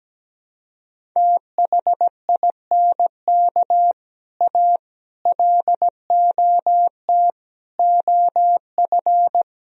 Morse code, heard as a keyed tone, about 17 wpm, 700 Hz. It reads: THINK A LOT OF
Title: Speed-Racing - Sets of 5 Words 17wpm